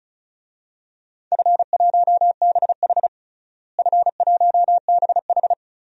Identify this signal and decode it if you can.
Morse code, heard as a keyed tone, about 36 wpm, 700 Hz. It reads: F1BH F1BH
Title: DXCC-100 Call Signs - 4 Characters - Repeated 2x - 35wpm